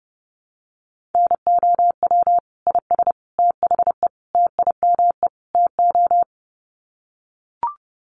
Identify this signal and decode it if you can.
Morse code, heard as a keyed tone, about 30 wpm, 700 Hz. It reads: NOW IS THE TIME TO
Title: Sets of 5 Words Encore2 30wpm